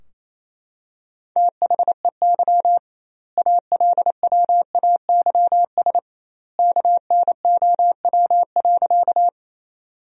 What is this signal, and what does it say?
Morse code, heard as a keyed tone, about 28 wpm, 700 Hz. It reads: THEY ALWAYS KNOW.
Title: 2x Word-Spacing - Sentences from Top 300 Words - 28wpm